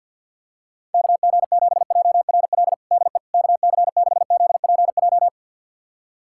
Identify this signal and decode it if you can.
Morse code, heard as a keyed tone, about 53 wpm, 700 Hz. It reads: KG7JRL DE KC6ZPJ
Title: USA Callsign DE Callsign - 6 Characters - 50wpm